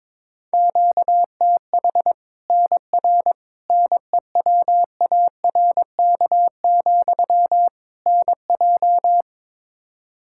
Morse code, heard as a keyed tone, about 22 wpm, 700 Hz. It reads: QTH NR NEWARK, NJ
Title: Practice QSO - Exchange 1 - Beginning - 22wpm